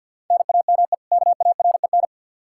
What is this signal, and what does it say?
Morse code, heard as a keyed tone, about 47 wpm, 700 Hz. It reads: NAME KAREN
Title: Practice QSO - Exchange 1 - Beginning - 50wpm